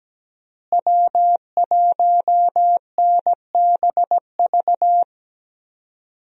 Morse code, heard as a keyed tone, about 17 wpm, 700 Hz. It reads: W1NBV
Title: USA Call Signs - 5 Characters - Repeated 2x - 17wpm